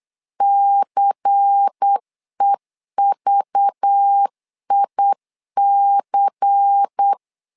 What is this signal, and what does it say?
Morse code, heard as a keyed tone, about 8 wpm, 800 Hz. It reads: CEVIC
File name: LEZIONI / 5 LBVC / 5 LBVC 8wpm 2935